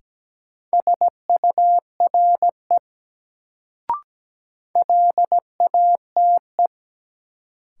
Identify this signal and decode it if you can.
Morse code, heard as a keyed tone, about 17 wpm, 700 Hz. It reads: SURE LATE
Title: Sets of 3 Words - New Word Drill - 17wpm